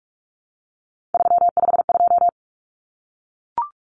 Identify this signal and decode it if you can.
Morse code, heard as a keyed tone, about 44 wpm, 700 Hz. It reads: 352
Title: Numbers - 3-Digit 45wpm